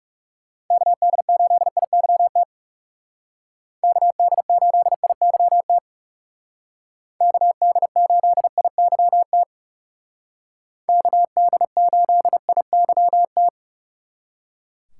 Morse code, heard as a keyed tone, about 36 wpm, 700 Hz. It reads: KD8IYT KD8IYT KD8IYT KD8IYT
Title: Speed Racing - USA Call Signs 30wpm